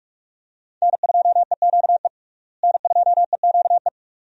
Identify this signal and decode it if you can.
Morse code, heard as a keyed tone, about 41 wpm, 700 Hz. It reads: N2EQE N2EQE
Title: USA Call Signs - 5 Characters - Repeated 2x - 45wpm